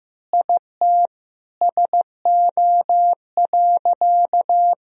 Morse code, heard as a keyed tone, about 15 wpm, 700 Hz. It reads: IT SO.
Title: Sentences from Top 1000 Words - Review Part 4 - 15wpm